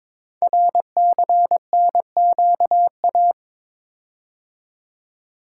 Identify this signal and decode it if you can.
Morse code, heard as a keyed tone, about 22 wpm, 700 Hz. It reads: RCNQA
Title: Code Groups - 5 Letters - 22wpm